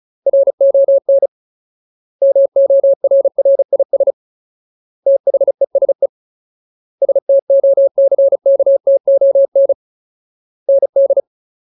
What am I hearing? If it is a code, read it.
Morse code, heard as a keyed tone, about 34 wpm, 550 Hz. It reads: RON MORRIS THESE STOCKTON ND
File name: Ultimate Word List / 35WPM Ultimate